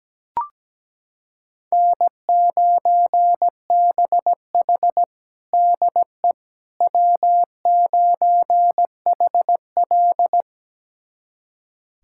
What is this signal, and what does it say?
Morse code, heard as a keyed tone, about 17 wpm, 700 Hz. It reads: N9BH DE W9HL
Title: USA Callsign DE Callsign - 4 Characters - 17wpm